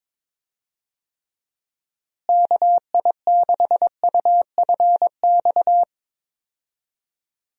Morse code, heard as a keyed tone, about 21 wpm, 700 Hz. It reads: KI6UFX
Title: USA Call Signs - 6 Characters - Repeated 2x - 22wpm